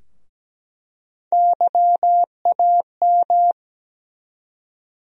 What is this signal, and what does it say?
Morse code, heard as a keyed tone, about 17 wpm, 700 Hz. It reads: YAM
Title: USA Call Sign Postfixes - 3 Letters 17wpm